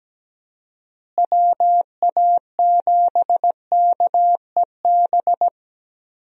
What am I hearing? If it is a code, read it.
Morse code, heard as a keyed tone, about 17 wpm, 700 Hz. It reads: WA7KEB
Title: USA Call Signs - 6 Characters - Repeated 2x - 17wpm